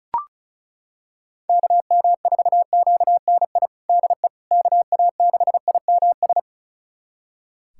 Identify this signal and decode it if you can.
Morse code, heard as a keyed tone, about 34 wpm, 700 Hz. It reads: KM4QNI DE KA6IMS
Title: USA Callsign DE Callsign - 6 Characters - 35wpm